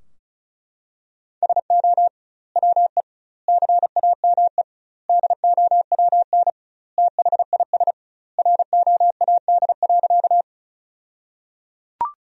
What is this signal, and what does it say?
Morse code, heard as a keyed tone, about 36 wpm, 700 Hz. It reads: SO WE CAME DOWN THIS ROAD.
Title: 2x Word-Spacing - Sentences from Top 400 Words - 35wpm